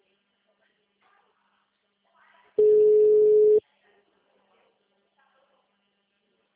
Ringback tone (Europe, 425 Hz).
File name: out- -2002- - - .77